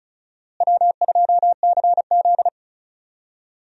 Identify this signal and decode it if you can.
Morse code, heard as a keyed tone, about 36 wpm, 700 Hz. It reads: W2CZ
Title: USA Call Signs - 4 Characters - Repeated 2x - 35wpm